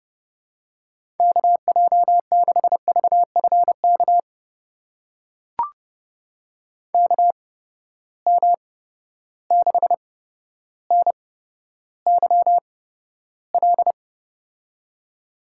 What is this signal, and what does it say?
Morse code, heard as a keyed tone, about 31 wpm, spaced as for 13 wpm, 700 Hz. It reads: KJ6VFK KM6NYL
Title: USA Call Signs - 6 Characters - 8x Character Spacing - 30wpm